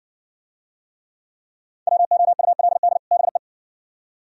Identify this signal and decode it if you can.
Morse code, heard as a keyed tone, about 53 wpm, 700 Hz. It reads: WOULD BE